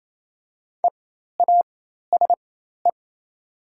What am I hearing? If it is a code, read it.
Morse code, heard as a keyed tone, about 28 wpm, spaced as for 13 wpm, 700 Hz. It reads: EASE
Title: 4x Character Spacing - Top 300 Words - 28wpm